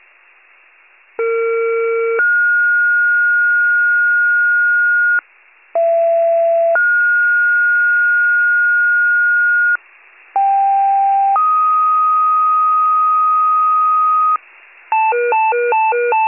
Two-tone page, heard as a one-second tone, then a three-second tone, about 460 then 1430 Hz, about 670 then 1430 Hz, about 790 then 1250 Hz, with noise. It is heard from 1.2 to 14.4 s.
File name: wcalert